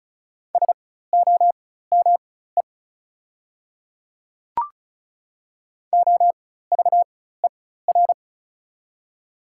Morse code, heard as a keyed tone, about 34 wpm, spaced as for 16 wpm, 700 Hz. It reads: SOME OVER
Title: 4x Character Spacing - Top 100 Words - 35wpm